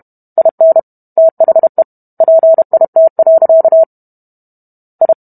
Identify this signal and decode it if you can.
Morse code, heard as a keyed tone, about 32 wpm, 650 Hz. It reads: IN THE PIT. I